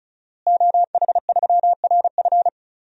Morse code, heard as a keyed tone, about 36 wpm, 700 Hz. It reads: OH3RF
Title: DXCC-100 Call Signs - 5 Characters - Repeated 2x - 35wpm